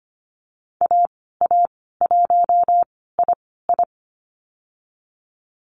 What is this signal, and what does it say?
Morse code, heard as a keyed tone, about 25 wpm, spaced as for 16 wpm, 700 Hz. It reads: AA1II
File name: Lesson 15g - Number 5 - Call Signs 25wpm